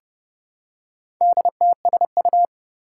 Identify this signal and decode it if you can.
Morse code, heard as a keyed tone, about 30 wpm, 700 Hz. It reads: DTSU
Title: Four Challenging Letter-Number Combinations 30wpm